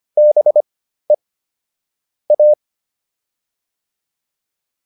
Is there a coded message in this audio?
Yes: Morse code from 0.2 to 2.5 s, heard as a keyed tone, about 25 wpm, spaced as for 13 wpm, 600 Hz. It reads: BE A